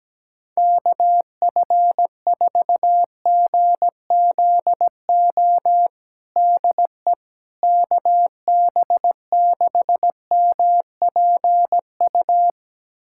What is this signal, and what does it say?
Morse code, heard as a keyed tone, about 17 wpm, 700 Hz. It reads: KF4GZO DE KB6MPU
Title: USA Callsign DE Callsign - 6 Characters - 17wpm